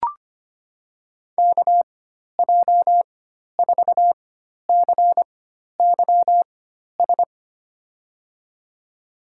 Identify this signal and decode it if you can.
Morse code, heard as a keyed tone, about 25 wpm, spaced as for 12 wpm, 700 Hz. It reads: KJ4CYS
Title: USA Call Signs - 6 Characters - 4x Character Spacing - 25wpm